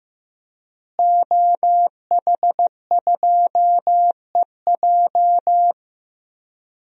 Morse code, heard as a keyed tone, about 15 wpm, 700 Hz. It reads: OH2EJ
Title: DXCC-100 Call Signs - 5 Characters - Repeated 3x - 15wpm